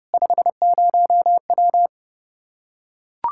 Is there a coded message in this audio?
Yes: Morse code from 0.1 to 1.9 s, heard as a keyed tone, about 30 wpm, 700 Hz. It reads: 50W